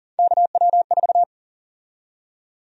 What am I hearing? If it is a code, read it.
Morse code, heard as a keyed tone, about 40 wpm, 700 Hz. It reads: KW4
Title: USA Call Sign Prefixes - 40wpm